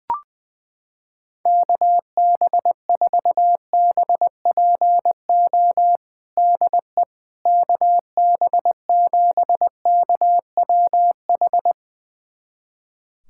Morse code, heard as a keyed tone, about 21 wpm, 700 Hz. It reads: KB4BPO DE KB7KWH
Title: USA Callsign DE Callsign - 6 Characters - 20wpm